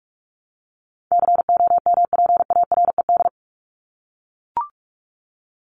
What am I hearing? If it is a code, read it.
Morse code, heard as a keyed tone, about 44 wpm, 700 Hz. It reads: COMPARED
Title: ED Suffix - Top 100 Words with Base Words 45wpm